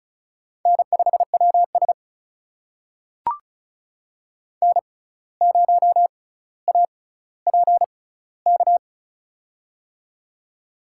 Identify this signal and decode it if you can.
Morse code, heard as a keyed tone, about 36 wpm, spaced as for 17 wpm, 700 Hz. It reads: N5WS N0APK
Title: USA Call Signs - 6x Character Spacing - 35wpm